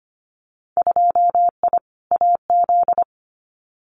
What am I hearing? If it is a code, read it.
Morse code, heard as a keyed tone, about 25 wpm, 700 Hz. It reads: 2I AZ
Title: Winter Field Day Exchanges 25wpm